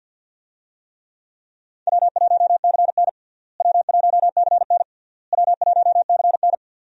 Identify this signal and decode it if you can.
Morse code, heard as a keyed tone, about 45 wpm, 700 Hz. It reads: W1CN W1CN W1CN
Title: USA Call Signs - 4 Characters - Repeated 3x - 50wpm